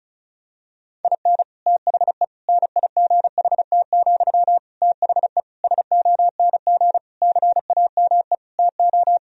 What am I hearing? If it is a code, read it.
Morse code, heard as a keyed tone, about 34 wpm, 700 Hz. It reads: IN THE NIGHT, THE SONG CAME TO
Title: Sentences from Top 1000 Words - Review Part 2 - 35wpm